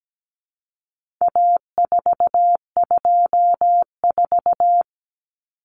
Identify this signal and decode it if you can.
Morse code, heard as a keyed tone, about 17 wpm, 700 Hz. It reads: A424
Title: Common Cut Numbers - 4-Digit 17wpm